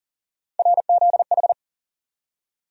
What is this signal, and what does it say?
Morse code, heard as a keyed tone, about 40 wpm, 700 Hz. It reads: RZH
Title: USA Call Sign Postfixes - 3 Letters 40wpm